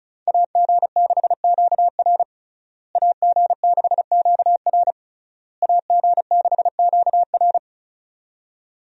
Morse code, heard as a keyed tone, about 34 wpm, 700 Hz. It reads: AG6QR AG6QR AG6QR
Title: USA Call Signs - 5 Characters - Repeated 3x - 35wpm